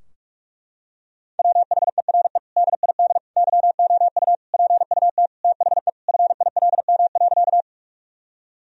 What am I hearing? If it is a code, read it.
Morse code, heard as a keyed tone, about 43 wpm, 700 Hz. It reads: WHERE DID YOU PUT THE FILM.
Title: Sentences from Top 1000 Words - Review Part 3 - 45wpm